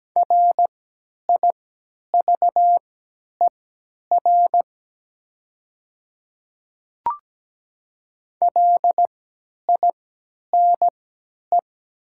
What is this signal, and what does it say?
Morse code, heard as a keyed tone, about 17 wpm, spaced as for 10 wpm, 700 Hz. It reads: RIVER LINE
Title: 3x Character Spacing - Top 500 Words - 17wpm